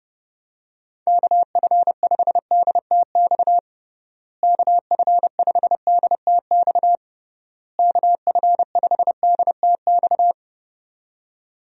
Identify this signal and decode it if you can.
Morse code, heard as a keyed tone, about 30 wpm, 700 Hz. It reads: KF5DTX KF5DTX KF5DTX
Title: USA Call Signs - 6 Characters - Repeated 3x - 30wpm